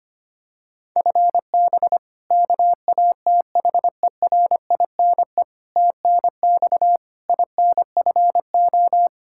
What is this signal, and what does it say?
Morse code, heard as a keyed tone, about 25 wpm, 700 Hz. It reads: FB KATHERINE TNX INFO
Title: Practice QSO - Exchange 3 - More Info - 25wpm